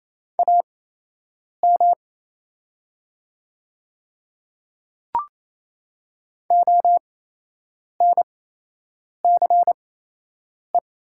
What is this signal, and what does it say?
Morse code, heard as a keyed tone, about 28 wpm, spaced as for 8 wpm, 700 Hz. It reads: AM ONCE
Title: 8x Character Spacing - Top 500 Words - 28wpm